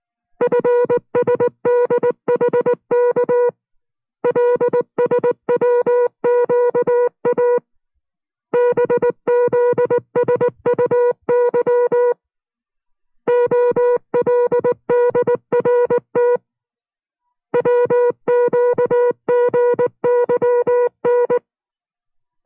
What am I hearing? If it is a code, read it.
Morse code, heard as a keyed tone, about 19 wpm, 490 Hz. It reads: FSDHK LSWQA BZSUY OLDRT WQGYN